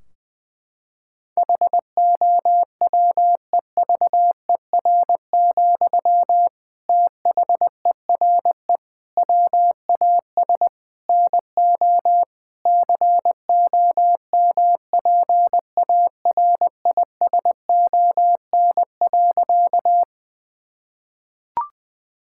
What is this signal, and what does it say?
Morse code, heard as a keyed tone, about 20 wpm, 700 Hz. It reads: HOWEVER, THERE WAS NO COMPARISON.
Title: Sentences from Top 1600 Words 20wpm